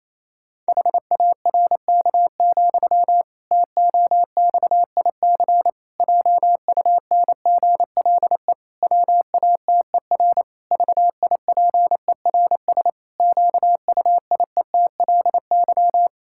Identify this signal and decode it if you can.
Morse code, heard as a keyed tone, about 28 wpm, 700 Hz. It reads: HARK, TOXIC JUNGLE WATER VIPERS QUIETLY
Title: Pangram Challenge v2 - 28wpm